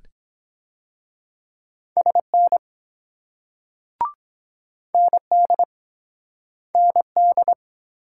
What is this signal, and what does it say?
Morse code, heard as a keyed tone, about 21 wpm, 700 Hz. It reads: SN ND ND
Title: Speed-Racing Multi-letter Phonemes - 17wpm